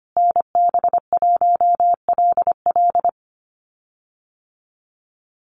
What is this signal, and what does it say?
Morse code, heard as a keyed tone, about 25 wpm, 700 Hz. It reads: NB1LL
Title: USA Call Signs Repeated 3x 25wpm